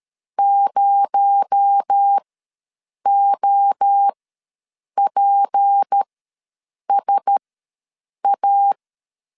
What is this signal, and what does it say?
Morse code, heard as a keyed tone, about 13 wpm, spaced as for 7 wpm, 800 Hz. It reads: 0OPSA
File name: LEZIONI / 7 GF9Z / 7 GF ws 12wpm 3140